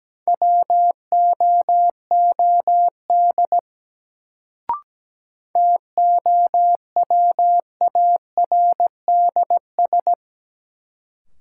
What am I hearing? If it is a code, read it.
Morse code, heard as a keyed tone, about 17 wpm, 700 Hz. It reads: WOOD TOWARDS
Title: Sentences from Top 1000 Words - Review Part 2 - New Word Drill - 17wpm